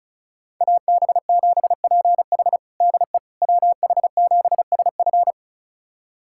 Morse code, heard as a keyed tone, about 36 wpm, 700 Hz. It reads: AB7PH DE WH7SF
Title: USA Callsign DE Callsign - 5 Characters - 35wpm